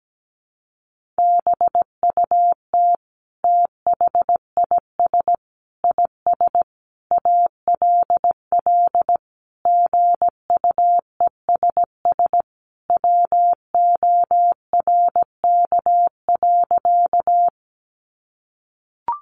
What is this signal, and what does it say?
Morse code, heard as a keyed tone, about 17 wpm, 700 Hz. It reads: BUT THIS IS ALL GUESS WORK.
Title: Sentences from Top 1500 Words 17wpm